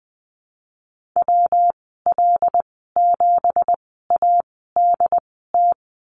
Morse code, heard as a keyed tone, about 20 wpm, spaced as for 15 wpm, 700 Hz. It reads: WL7ADT
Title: 173 Lesson 28g - Number 9 - Call Signs 20wpm